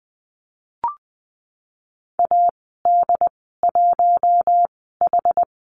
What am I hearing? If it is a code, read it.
Morse code, heard as a keyed tone, about 20 wpm, spaced as for 14 wpm, 700 Hz. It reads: AD1H
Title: 069 Lesson 12g - Letter d - Call Signs 20wpm